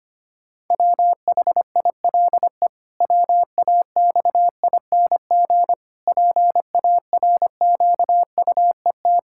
Morse code, heard as a keyed tone, about 25 wpm, 700 Hz. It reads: WHILE WAXING PARQUET
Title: Pangram Challenge v2 - 25wpm